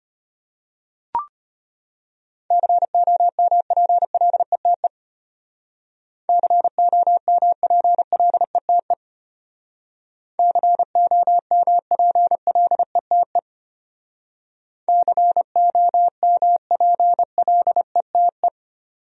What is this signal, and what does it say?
Morse code, heard as a keyed tone, about 32 wpm, 700 Hz. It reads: COMPLETE COMPLETE COMPLETE COMPLETE